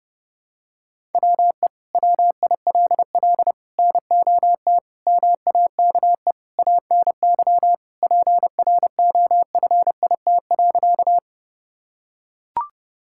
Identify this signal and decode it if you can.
Morse code, heard as a keyed tone, about 32 wpm, 700 Hz. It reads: WE WILL NOT MAKE ANY PROFIT.
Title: Sentences from Top 1000 Words - Review Part 5 - 30wpm